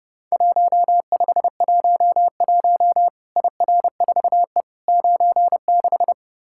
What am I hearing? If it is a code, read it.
Morse code, heard as a keyed tone, about 31 wpm, 700 Hz. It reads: 1511 IR4E 96
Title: Worked All Europe DX Contest - QTC Exchange - 30wpm